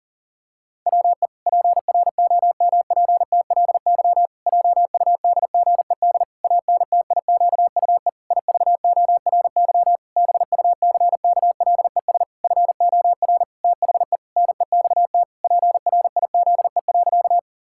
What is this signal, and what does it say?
Morse code, heard as a keyed tone, about 40 wpm, 700 Hz. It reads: WE PROMPTLY JUDGED ANTIQUE IVORY BUCKLES FOR THE NEXT PRIZE.
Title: Pangram Challenge v2 - 40wpm